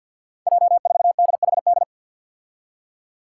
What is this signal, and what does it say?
Morse code, heard as a keyed tone, about 51 wpm, 700 Hz. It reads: J4DHD
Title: Five Challenging Letter-Number Combinations 50wpm